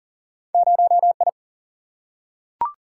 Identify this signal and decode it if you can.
Morse code, heard as a keyed tone, about 40 wpm, 700 Hz. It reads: 0 I
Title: Single-Letter Single-Number 40wpm